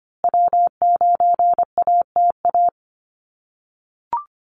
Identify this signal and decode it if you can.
Morse code, heard as a keyed tone, about 25 wpm, 700 Hz. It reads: W9ATA
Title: USA Call Signs Repeated 3x 25wpm